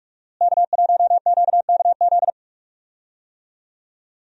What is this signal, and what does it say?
Morse code, heard as a keyed tone, about 46 wpm, 700 Hz. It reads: K1QKZ
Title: USA Call Signs - 5 Characters - Repeated 2x - 45wpm